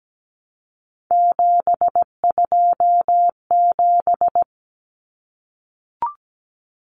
Morse code, heard as a keyed tone, about 17 wpm, 700 Hz. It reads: 727
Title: Numbers - 3-Digit 17wpm